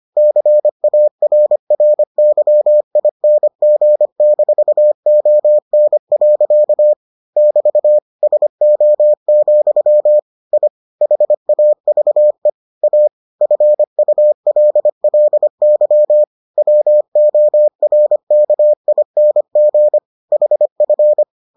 Morse code, heard as a keyed tone, about 25 wpm, 600 Hz. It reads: CARRYING-ON. <BT> SO, I HAVE A FULLY WORKING HF